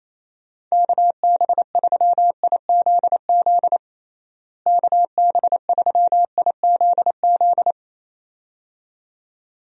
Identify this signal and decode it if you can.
Morse code, heard as a keyed tone, about 28 wpm, 700 Hz. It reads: KB3IZZ KB3IZZ
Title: USA Call Signs - 6 Characters - Repeated 2x - 28wpm